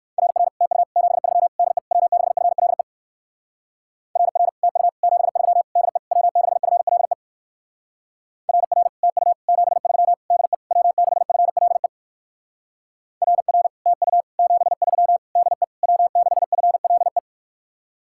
Morse code, heard as a keyed tone, about 59 wpm, 700 Hz. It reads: RR TU 73 DE W6FLE RR TU 73 DE W6FLE RR TU 73 DE W6FLE RR TU 73 DE W6FLE